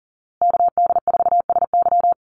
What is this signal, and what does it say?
Morse code, heard as a keyed tone, about 40 wpm, 700 Hz. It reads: KD4SY
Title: USA Call Signs Repeated 3x 40wpm